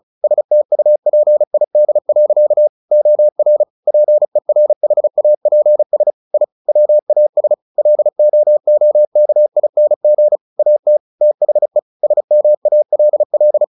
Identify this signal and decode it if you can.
Morse code, heard as a keyed tone, about 34 wpm, 600 Hz. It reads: STUPID. OR PERHAPS I WAS LOOKING AT THE SMALL
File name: the old man and the sea / the old man and the sea 1 35wpm